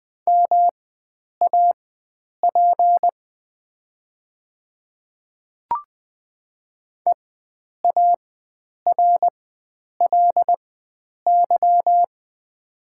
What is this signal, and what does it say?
Morse code, heard as a keyed tone, about 20 wpm, spaced as for 9 wpm, 700 Hz. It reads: MAP EARLY
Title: 4x Character Spacing - Top 500 Words - 20wpm